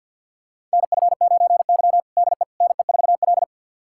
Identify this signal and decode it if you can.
Morse code, heard as a keyed tone, about 52 wpm, 700 Hz. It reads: NF9Y DE NE4L
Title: USA Callsign DE Callsign - 4 Characters - 50wpm